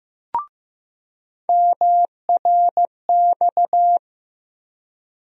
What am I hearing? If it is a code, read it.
Morse code, heard as a keyed tone, about 15 wpm, 700 Hz. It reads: MRX
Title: USA Call Sign Postfixes - 3 Letters 15wpm